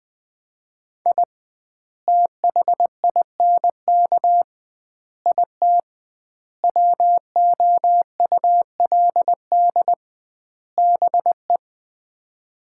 Morse code, heard as a keyed tone, about 21 wpm, 700 Hz. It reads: I THINK IT WOULD BE
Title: 2x Word-Spacing - Sets of 5 Words - 20wpm